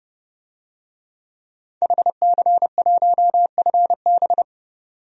Morse code, heard as a keyed tone, about 30 wpm, 700 Hz. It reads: HC1FB
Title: DXCC-100 Call Signs - 5 Characters - 30wpm